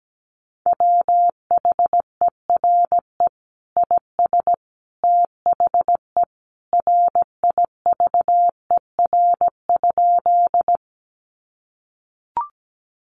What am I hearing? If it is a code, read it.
Morse code, heard as a keyed tone, about 17 wpm, 700 Hz. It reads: WHERE IS THE RIVER?
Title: Sentences from Top 300 Words 17wpm